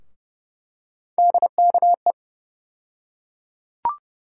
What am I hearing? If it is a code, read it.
Morse code, heard as a keyed tone, about 31 wpm, 700 Hz. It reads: DKE
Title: USA Call Sign Postfixes - 3 Letters 30wpm